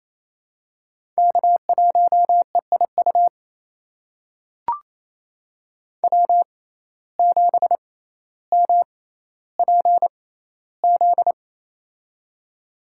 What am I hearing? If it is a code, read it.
Morse code, heard as a keyed tone, about 27 wpm, spaced as for 14 wpm, 700 Hz. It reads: K1EIU W7MPZ
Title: USA Call Signs - 5 Characters - 6x Character Spacing - 28wpm